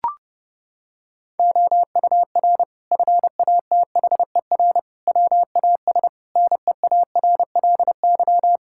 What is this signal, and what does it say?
Morse code, heard as a keyed tone, about 31 wpm, 700 Hz. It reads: OUR FATHER WAS NEARLY
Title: Sentences from Top 1600 Words 30wpm